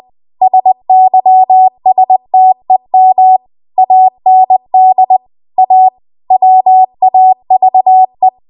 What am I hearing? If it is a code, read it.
Morse code, heard as a keyed tone, about 20 wpm, 750 Hz. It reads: SYSTEM AND A WAVE
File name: Archive / 20 WPM / 20WPM